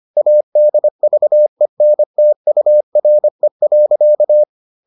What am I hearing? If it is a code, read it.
Morse code, heard as a keyed tone, about 25 wpm, 600 Hz. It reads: ADVENTURE.